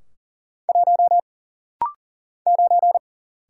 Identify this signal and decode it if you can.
Morse code, heard as a keyed tone, about 40 wpm, 700 Hz. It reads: 1 9
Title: Single Letter-Number - Rapid-Fire 40wpm